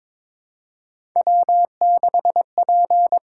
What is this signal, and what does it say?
Morse code, heard as a keyed tone, about 23 wpm, 700 Hz. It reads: W6P
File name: practice / 22wpm / Three Letter-Number Combinations - 22wpm